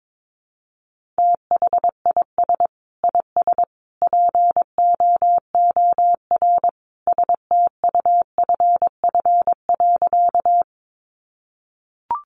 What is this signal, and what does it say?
Morse code, heard as a keyed tone, about 22 wpm, 700 Hz. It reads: THIS IS POOR STUFF.
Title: Sentences from Top 1400 Words 22wpm